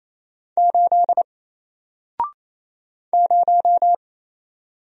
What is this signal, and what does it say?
Morse code, heard as a keyed tone, about 28 wpm, 700 Hz. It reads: 8 0
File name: Single-Letter-Number-28wpm